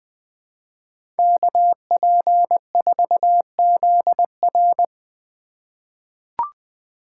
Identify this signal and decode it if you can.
Morse code, heard as a keyed tone, about 20 wpm, 700 Hz. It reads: KP4ZR
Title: USA Call Signs - 5 Characters - 4x Character Spacing - 20wpm